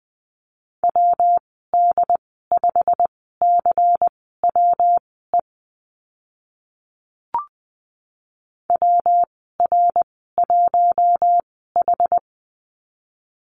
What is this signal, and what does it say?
Morse code, heard as a keyed tone, about 20 wpm, spaced as for 14 wpm, 700 Hz. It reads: WD5CWE WR1H
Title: 140 Lesson 23g - Letter f - Call Signs 20wpm